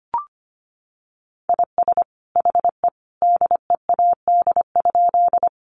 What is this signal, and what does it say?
Morse code, heard as a keyed tone, about 25 wpm, 700 Hz. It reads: IS HE DEAD?
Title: Sentences from Top 1500 Words 25wpm